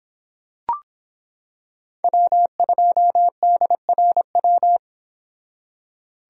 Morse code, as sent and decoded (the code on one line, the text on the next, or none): .-- ..--- -.. .-. .--
W2DRW